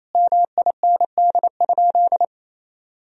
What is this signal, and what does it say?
Morse code, heard as a keyed tone, about 28 wpm, 700 Hz. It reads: MIND?
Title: Sentences from Top 1000 Words - Review Part 1 - 28wpm